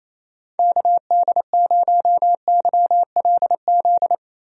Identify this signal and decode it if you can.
Morse code, heard as a keyed tone, about 28 wpm, 700 Hz. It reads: KD0YLZ
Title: USA Call Signs - 6 Characters - Repeated 2x - 28wpm